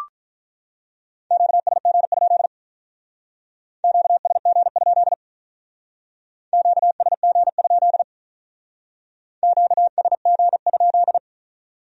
Morse code, heard as a keyed tone, about 45 wpm, 700 Hz. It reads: QSG? QSG? QSG? QSG?